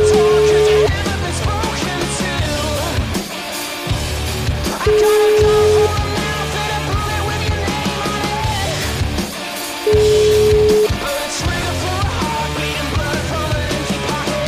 Ringback tone (Europe, 425 Hz), with noise.